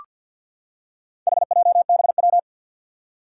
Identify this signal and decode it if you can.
Morse code, heard as a keyed tone, about 53 wpm, 700 Hz. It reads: HJBW
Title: Four Challenging Letter-Number Combinations 50wpm